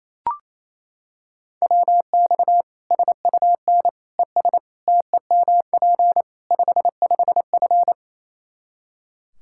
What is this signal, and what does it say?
Morse code, heard as a keyed tone, about 28 wpm, 700 Hz. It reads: WX SUN ES TEMP 55F
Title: Practice QSO - Exchange 2 - Info - 28wpm